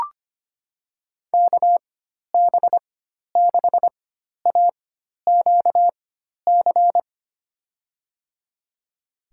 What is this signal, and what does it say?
Morse code, heard as a keyed tone, about 25 wpm, spaced as for 12 wpm, 700 Hz. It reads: KB6AQC
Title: USA Call Signs - 6 Characters - 4x Character Spacing - 25wpm